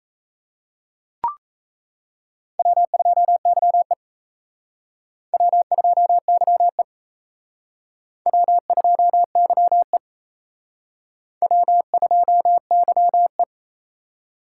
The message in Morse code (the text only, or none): W2YE W2YE W2YE W2YE